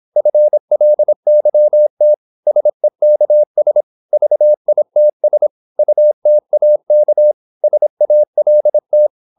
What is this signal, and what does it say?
Morse code, heard as a keyed tone, about 26 wpm, 600 Hz. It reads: FLYT SEKS VITS UTAK SALT
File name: Korte ord DK 26wpm